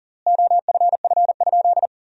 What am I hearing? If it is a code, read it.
Morse code, heard as a keyed tone, about 40 wpm, 700 Hz. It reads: OFF?
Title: 3x Word-Spacing - Sentences from Top 500 Words - 40wpm